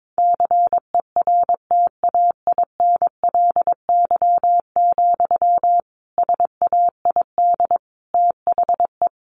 Morse code, heard as a keyed tone, about 22 wpm, 700 Hz. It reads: CERTAINLY, SAID THE
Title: Sentences from Top 1500 Words 22wpm